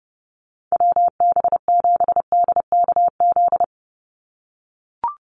Morse code, heard as a keyed tone, about 30 wpm, 700 Hz. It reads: WB7DKZ
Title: Call Signs - 30wpm